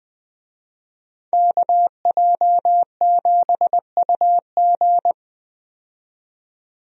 Morse code, heard as a keyed tone, about 20 wpm, 700 Hz. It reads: KJ7UG
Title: USA Call Signs - 5 Characters - 20wpm